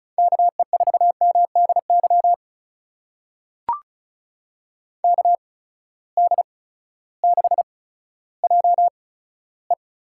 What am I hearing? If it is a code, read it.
Morse code, heard as a keyed tone, about 34 wpm, spaced as for 16 wpm, 700 Hz. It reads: KE4MDY KD6JE
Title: USA Call Signs - 6 Characters - 8x Character Spacing - 35wpm